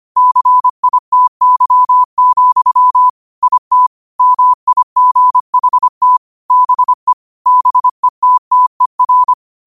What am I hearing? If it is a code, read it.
Morse code, heard as a keyed tone, about 25 wpm, 1000 Hz. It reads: CITY, IT MIGHT BE BETTER
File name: qotdfiles / 25wpm